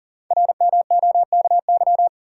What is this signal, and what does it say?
Morse code, heard as a keyed tone, about 39 wpm, 690 Hz. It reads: RMOKY
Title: Code Groups - 5 Letters - 40wpm